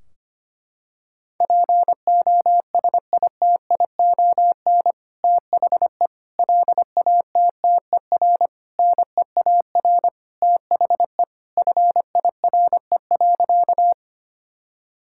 Morse code, heard as a keyed tone, about 25 wpm, 700 Hz. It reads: POSITION THE LATTER NEAR THE FIRE.
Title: Sentences from Top 1600 Words 25wpm